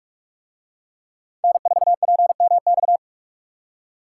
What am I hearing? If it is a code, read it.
Morse code, heard as a keyed tone, about 48 wpm, 700 Hz. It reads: N4PMX